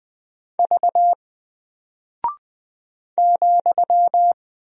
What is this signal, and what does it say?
Morse code, heard as a keyed tone, about 20 wpm, 700 Hz. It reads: V ,
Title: Single Character (Letter:Number:Punctuation:Prosign) 20wpm